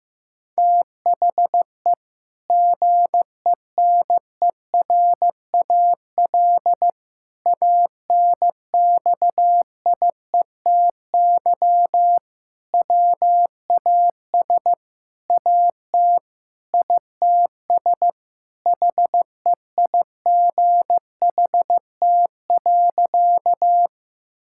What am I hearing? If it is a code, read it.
Morse code, heard as a keyed tone, about 15 wpm, 700 Hz. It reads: THE GENERAL ANXIETY WAS AT ITS HEIGHT.
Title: Sentences from Top 1600 Words 15wpm